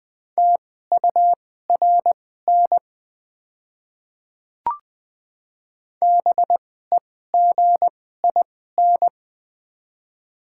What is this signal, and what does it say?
Morse code, heard as a keyed tone, about 21 wpm, spaced as for 15 wpm, 700 Hz. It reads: TURN BEGIN
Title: 2x Character Spacing - Top 300 Words - 20wpm